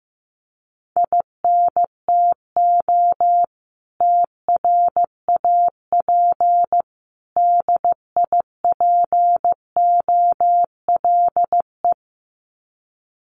Morse code, heard as a keyed tone, about 15 wpm, 700 Hz. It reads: INTO TRAP DIPOLE
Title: Common QSO Phrases 15wpm